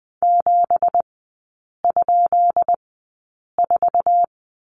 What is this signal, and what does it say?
Morse code, heard as a keyed tone, about 20 wpm, spaced as for 8 wpm, 700 Hz. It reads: 7?4
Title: 166 Lesson 27e - Number 7 - Three Characters 20wpm